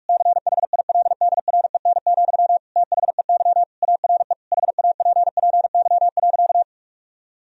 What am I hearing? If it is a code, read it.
Morse code, heard as a keyed tone, about 46 wpm, 700 Hz. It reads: CHILDREN, THEY ARE HAPPY.
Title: Sentences from Top 1000 Words - Review Part 3 - 45wpm